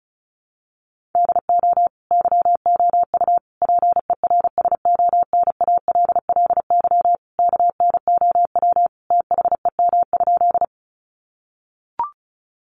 Morse code, heard as a keyed tone, about 34 wpm, 700 Hz. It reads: DO YOU PERSONALLY KNOW THEM?
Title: Sentences from Top 1400 Words 35wpm